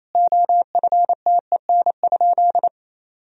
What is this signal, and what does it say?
Morse code, heard as a keyed tone, about 28 wpm, 700 Hz. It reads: OFTEN?
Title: Sentences from Top 1000 Words - Review Part 4 - 28wpm